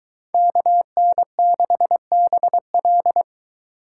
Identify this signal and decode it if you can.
Morse code, heard as a keyed tone, about 23 wpm, 700 Hz. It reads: KN6BL